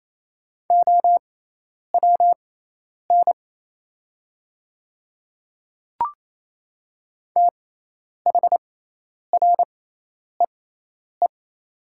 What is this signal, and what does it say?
Morse code, heard as a keyed tone, about 27 wpm, spaced as for 10 wpm, 690 Hz. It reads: OWN THREE
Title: 6x Character Spacing - Top 500 Words - 28wpm